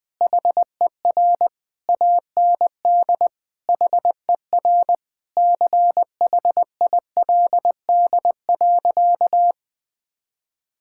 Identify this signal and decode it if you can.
Morse code, heard as a keyed tone, about 20 wpm, 700 Hz. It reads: HER AND HER CHILD.
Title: Sentences from Top 1600 Words 20wpm